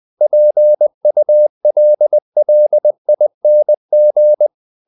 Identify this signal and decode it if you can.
Morse code, heard as a keyed tone, about 20 wpm, 600 Hz. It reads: PULLING
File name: the old man and the sea part 2 20wpm